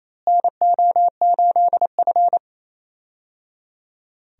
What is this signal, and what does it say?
Morse code, heard as a keyed tone, about 28 wpm, 700 Hz. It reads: NO8F
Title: USA Call Signs - 4 Characters - Repeated 3x - 28wpm